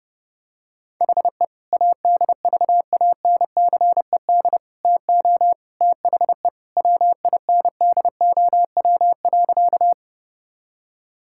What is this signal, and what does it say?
Morse code, heard as a keyed tone, about 30 wpm, 700 Hz. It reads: HE ADVANCED TO THE WINDOW.
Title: Sentences from Top 1600 Words 30wpm